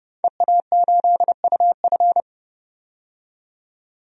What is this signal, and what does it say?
Morse code, heard as a keyed tone, about 30 wpm, 700 Hz. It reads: EA8UF
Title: DXCC-100 Call Signs - 5 Characters - 30wpm